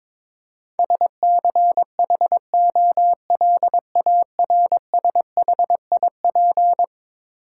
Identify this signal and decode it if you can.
Morse code, heard as a keyed tone, about 21 wpm, 700 Hz. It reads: SCHOLARSHIP
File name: practice / 22wpm / Eleven Letter Words - 22wpm